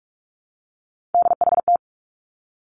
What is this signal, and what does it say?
Morse code, heard as a keyed tone, about 46 wpm, 700 Hz. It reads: DHT
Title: Three Challenging Letter-Number Combinations 45wpm